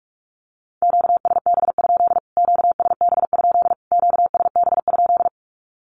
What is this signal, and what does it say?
Morse code, heard as a keyed tone, about 44 wpm, 700 Hz. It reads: QSB? QSB? QSB?
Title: Q-Codes Repeated 3x 45wpm